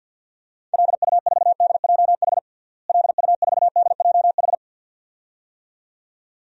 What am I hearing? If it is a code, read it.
Morse code, heard as a keyed tone, about 52 wpm, 700 Hz. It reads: LU4DJH LU4DJH
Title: DXCC-100 Call Signs - 6 Characters - Repeated 2x - 50wpm